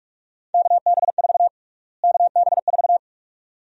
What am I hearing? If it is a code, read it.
Morse code, heard as a keyed tone, about 46 wpm, 700 Hz. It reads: KB4 KB4
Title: USA Call Sign Prefixes - Repeated 2x - 45wpm